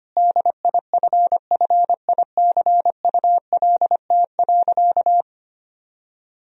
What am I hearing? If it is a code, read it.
Morse code, heard as a keyed tone, about 25 wpm, 700 Hz. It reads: DIFFICULT.
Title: Sentences from Top 1000 Words - Review Part 2 - 25wpm